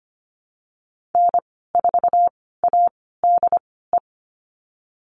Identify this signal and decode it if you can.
Morse code, heard as a keyed tone, about 25 wpm, spaced as for 16 wpm, 700 Hz. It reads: N4ADE
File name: Lesson 15g - Number 5 - Call Signs 25wpm